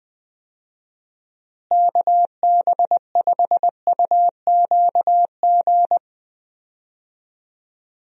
Morse code, heard as a keyed tone, about 20 wpm, 700 Hz. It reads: KB5UQG
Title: USA Call Signs - 6 Characters - Repeated 3x - 20wpm